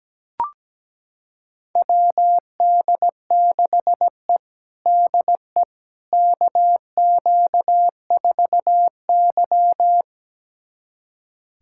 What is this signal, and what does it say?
Morse code, heard as a keyed tone, about 17 wpm, 700 Hz. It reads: WD6E DE KQ4Y
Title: USA Callsign DE Callsign - 4 Characters - 17wpm